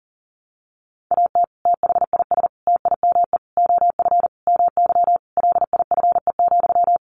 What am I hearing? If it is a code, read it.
Morse code, heard as a keyed tone, about 37 wpm, 700 Hz. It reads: AT THIS TIME OF MY LIFE,
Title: Sentences from Top 300 Words 40wpm